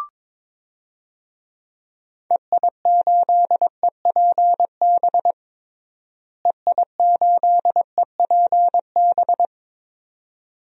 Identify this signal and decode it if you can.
Morse code, heard as a keyed tone, about 22 wpm, 700 Hz. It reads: EI8EPB EI8EPB
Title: DXCC-100 Call Signs - 6 Characters - Repeated 2x - 22wpm